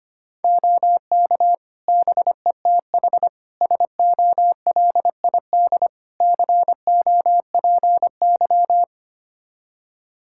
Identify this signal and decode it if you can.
Morse code, heard as a keyed tone, about 25 wpm, 700 Hz. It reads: OK BETH SOLID COPY
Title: Practice QSO - Exchange 3 - More Info - 25wpm